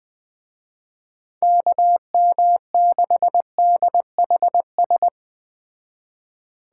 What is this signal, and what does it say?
Morse code, heard as a keyed tone, about 20 wpm, 700 Hz. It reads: KM6DHS
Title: Speed Racing - USA 2x3 Call Signs 17wpm